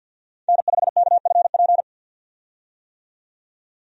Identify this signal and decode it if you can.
Morse code, heard as a keyed tone, about 46 wpm, 700 Hz. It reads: N5KFP
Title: USA Call Signs - 5 Characters - 50wpm